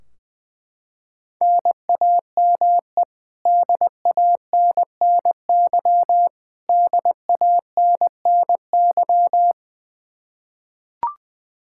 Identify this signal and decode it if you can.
Morse code, heard as a keyed tone, about 21 wpm, 700 Hz. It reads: NAME DANNY DANNY
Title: Practice QSO - Exchange 1 - Beginning - 20wpm